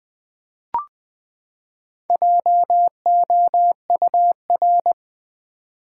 Morse code, heard as a keyed tone, about 20 wpm, 700 Hz. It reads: JOUR
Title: Build A Word - 4 Letters At A Time - 10-Letter Words - 20wpm